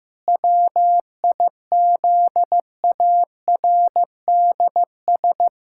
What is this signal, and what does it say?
Morse code, heard as a keyed tone, about 15 wpm, 700 Hz. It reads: WIZARDS
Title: Pangram Challenge v2 - 15wpm